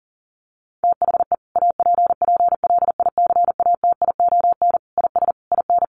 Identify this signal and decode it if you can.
Morse code, heard as a keyed tone, about 40 wpm, 700 Hz. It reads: THE APPLICATION IS IN
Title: Sentences from Top 1200 Words 40wpm